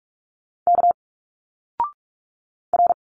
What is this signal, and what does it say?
Morse code, heard as a keyed tone, about 46 wpm, 700 Hz. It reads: K R
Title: Single Letters 45wpm